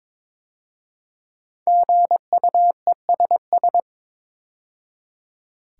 Morse code, heard as a keyed tone, about 22 wpm, 690 Hz. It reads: GUESS
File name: 5 Letter Words - First 3 Letters Given - 22wpm